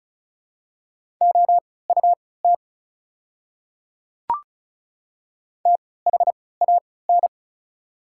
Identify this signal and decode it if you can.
Morse code, heard as a keyed tone, about 36 wpm, spaced as for 20 wpm, 690 Hz. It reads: OUT THAN
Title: 3x Character Spacing - Top 100 Words - 35wpm